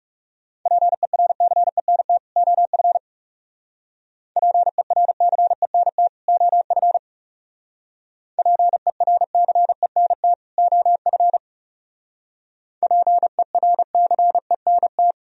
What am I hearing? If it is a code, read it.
Morse code, heard as a keyed tone, about 37 wpm, 700 Hz. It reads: PERCENT OF PERCENT OF PERCENT OF PERCENT
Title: Speed-Racing - Sets of 2 Words 30wpm